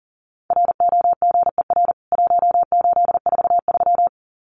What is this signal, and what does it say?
Morse code, heard as a keyed tone, about 40 wpm, 700 Hz. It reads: ROGER 1843
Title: CWT Exchange 40wpm